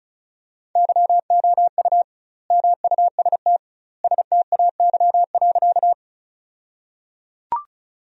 Morse code, heard as a keyed tone, about 34 wpm, 700 Hz. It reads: YOU MUST STAY.
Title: 2x Word-Spacing - Sentences from Top 500 Words - 35wpm